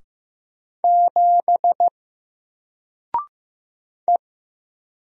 Morse code, heard as a keyed tone, about 15 wpm, 700 Hz. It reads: 7 E
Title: Single Character (Letter:Number:Punctuation:Prosign) 15wpm